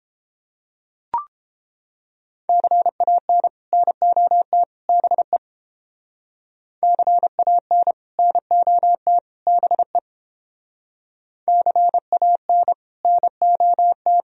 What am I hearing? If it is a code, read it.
Morse code, heard as a keyed tone, about 30 wpm, 700 Hz. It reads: CAN NOT BE CAN NOT BE CAN NOT